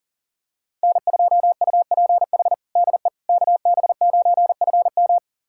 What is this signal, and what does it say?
Morse code, heard as a keyed tone, about 40 wpm, 700 Hz. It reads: N2UPH DE KB9FM
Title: USA Callsign DE Callsign - 5 Characters - 40wpm